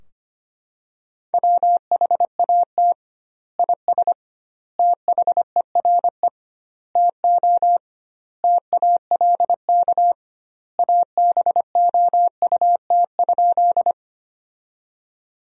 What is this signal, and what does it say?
Morse code, heard as a keyed tone, about 25 wpm, 700 Hz. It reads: WHAT IS THERE TO TALK ABOUT?
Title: 2x Word-Spacing - Sentences from Top 400 Words - 25wpm